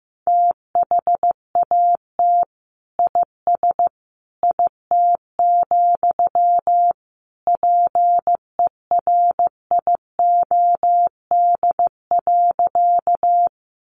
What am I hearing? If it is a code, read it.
Morse code, heard as a keyed tone, about 15 wpm, 700 Hz. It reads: THAT IS IT, PERIOD.
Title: Sentences from Top 800 Words 15wpm